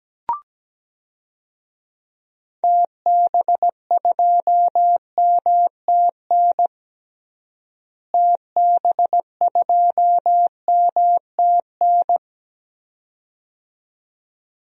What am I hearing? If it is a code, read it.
Morse code, heard as a keyed tone, about 17 wpm, 700 Hz. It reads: TB2MTN TB2MTN
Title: DXCC-100 Call Signs - 6 Characters - Repeated 2x - 17wpm